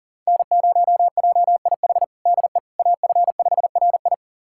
Morse code, heard as a keyed tone, about 40 wpm, 700 Hz. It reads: N0JIH DE AF5RI
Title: USA Callsign DE Callsign - 5 Characters - 40wpm